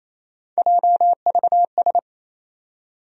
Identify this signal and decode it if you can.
Morse code, heard as a keyed tone, about 28 wpm, 700 Hz. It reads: JVS
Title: USA Call Sign Postfixes - 3 Letters 28wpm